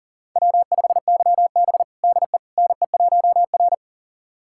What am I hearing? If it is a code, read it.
Morse code, heard as a keyed tone, about 40 wpm, 700 Hz. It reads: W5YB DE NE1R
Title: USA Callsign DE Callsign - 4 Characters - 40wpm